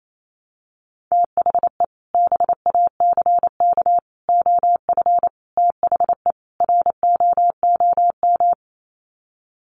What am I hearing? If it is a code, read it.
Morse code, heard as a keyed tone, about 28 wpm, 700 Hz. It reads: THE BACK OF THE ROOM
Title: Sets of 5 Words Encore 28wpm